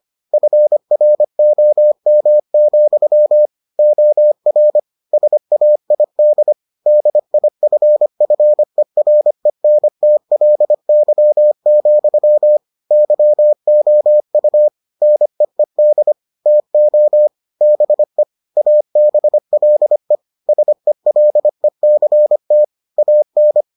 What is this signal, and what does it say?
Morse code, heard as a keyed tone, about 25 wpm, 600 Hz. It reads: FROM, OR SAID DIFFERENTLY, YOU NEED TO BE ABLE SELECT AN